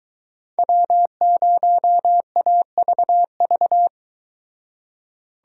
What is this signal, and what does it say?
Morse code, heard as a keyed tone, about 22 wpm, 700 Hz. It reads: W0AVV